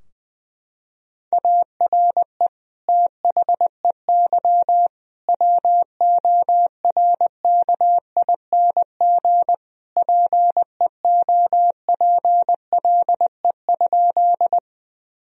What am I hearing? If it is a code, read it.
Morse code, heard as a keyed tone, about 20 wpm, 700 Hz. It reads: ARE THEY WORKING PEOPLE?
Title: Sentences from Top 1000 Words - Review Part 4 - 20wpm